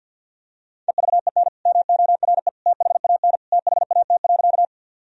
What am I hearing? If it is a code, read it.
Morse code, heard as a keyed tone, about 50 wpm, 700 Hz. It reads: EVEN MORE THAN THAT.
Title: Sentences from Top 1000 Words - Review Part 1 - 50wpm